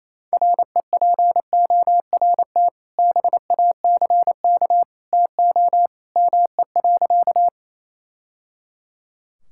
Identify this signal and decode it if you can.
Morse code, heard as a keyed tone, about 28 wpm, 700 Hz. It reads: REPORT BACK TO ME.
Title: Sentences from Top 1000 Words - Review Part 6 - 28wpm